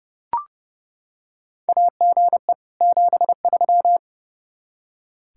Morse code, heard as a keyed tone, about 30 wpm, 700 Hz. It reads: AGE 73
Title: Practice QSO - Exchange 3 - More Info - 30wpm